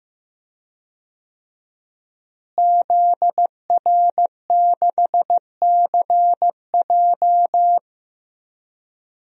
Morse code, heard as a keyed tone, about 16 wpm, 700 Hz. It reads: ZR6CJ
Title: DXCC-100 Call Signs - 5 Characters - Repeated 2x - 15wpm